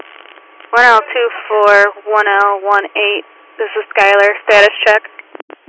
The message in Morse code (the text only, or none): none